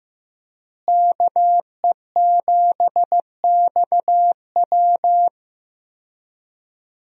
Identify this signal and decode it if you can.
Morse code, heard as a keyed tone, about 15 wpm, 700 Hz. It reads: KE7XW
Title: USA Call Signs - 5 Characters - Repeated 2x - 15wpm